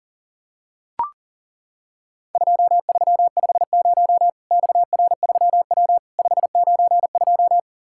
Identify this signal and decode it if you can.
Morse code, heard as a keyed tone, about 40 wpm, 700 Hz. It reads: 2350 XR3W 592
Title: Worked All Europe DX Contest - QTC Exchange - 40wpm